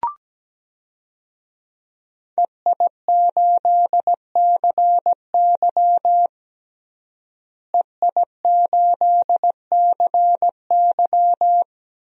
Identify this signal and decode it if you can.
Morse code, heard as a keyed tone, about 17 wpm, 700 Hz. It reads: EI8CY EI8CY
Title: DXCC-100 Call Signs - 5 Characters - Repeated 3x - 17wpm